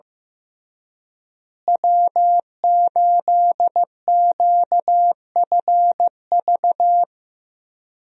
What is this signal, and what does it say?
Morse code, heard as a keyed tone, about 15 wpm, 700 Hz. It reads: W8QFV
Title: USA Call Signs - 5 Characters - Repeated 3x - 15wpm